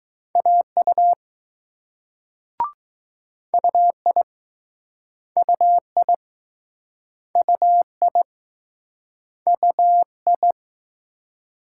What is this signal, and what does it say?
Morse code, heard as a keyed tone, about 20 wpm, 700 Hz. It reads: AU UI UI UI UI